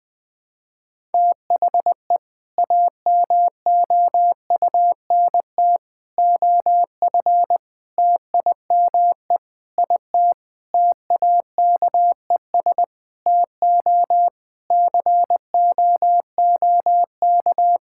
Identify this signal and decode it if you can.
Morse code, heard as a keyed tone, about 20 wpm, 700 Hz. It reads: THE AMOUNT OF TIME IT TAKES TO COOK
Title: Sentences with 7-Word Phrases - Spoken Beginning - 20wpm